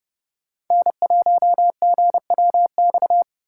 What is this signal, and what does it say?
Morse code, heard as a keyed tone, about 28 wpm, 700 Hz. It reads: N1GWX